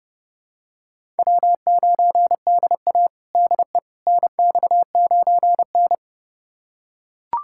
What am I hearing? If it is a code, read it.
Morse code, heard as a keyed tone, about 30 wpm, 700 Hz. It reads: W9DA DE NX9N
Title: USA Callsign DE Callsign - 4 Characters - 30wpm